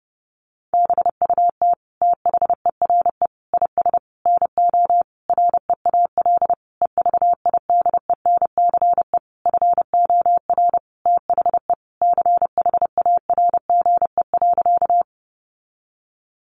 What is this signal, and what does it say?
Morse code, heard as a keyed tone, about 30 wpm, 700 Hz. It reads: BUT THERE IS NO REAL EVIDENCE FOR THE CHARGE.
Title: Sentences from Top 1300 Words 30wpm